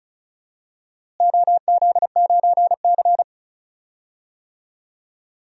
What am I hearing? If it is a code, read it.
Morse code, heard as a keyed tone, about 36 wpm, 700 Hz. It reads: OZ9C
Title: DXCC-100 Call Signs - 4 Characters - Repeated 2x - 35wpm